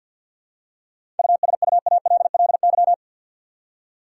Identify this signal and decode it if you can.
Morse code, heard as a keyed tone, about 51 wpm, 700 Hz. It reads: USUALLY
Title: Seven Letter Words - 50wpm